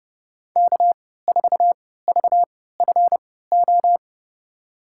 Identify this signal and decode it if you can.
Morse code, heard as a keyed tone, about 31 wpm, spaced as for 17 wpm, 700 Hz. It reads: K4VFO
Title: USA Call Signs - 3x Character Spacing - 30wpm